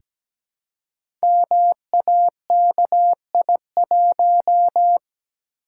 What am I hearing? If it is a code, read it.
Morse code, heard as a keyed tone, about 17 wpm, 700 Hz. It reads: MAKI1
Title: Code Groups - 5 Letter:Numbers - 17wpm